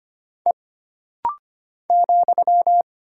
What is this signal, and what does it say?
Morse code, heard as a keyed tone, about 25 wpm, 700 Hz. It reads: E ,
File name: Single Character (Letter:Number:Punctuation:Prosign) - Rapid-Fire - 25wpm